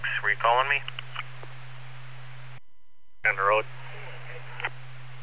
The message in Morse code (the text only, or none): none